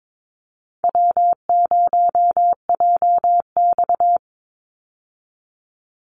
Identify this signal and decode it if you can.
Morse code, heard as a keyed tone, about 22 wpm, 700 Hz. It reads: W0JX
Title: Easy Call Signs - 22wpm